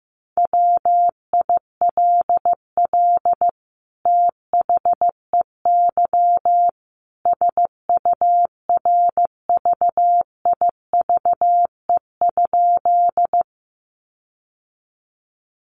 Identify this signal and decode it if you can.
Morse code, heard as a keyed tone, about 15 wpm, 700 Hz. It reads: WILL THEY SURVIVE?
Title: Sentences from Top 1300 Words 15wpm